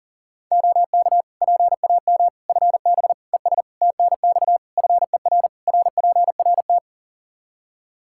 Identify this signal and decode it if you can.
Morse code, heard as a keyed tone, about 40 wpm, 700 Hz. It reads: OK PAM FB ES TNX FER RPRT
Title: K1USN SST QSO Practice 40wpm